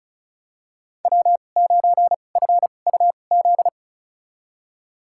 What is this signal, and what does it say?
Morse code, heard as a keyed tone, about 36 wpm, spaced as for 25 wpm, 700 Hz. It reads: W9FUZ
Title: USA Call Signs - 5 Characters - 2x Character Spacing - 35wpm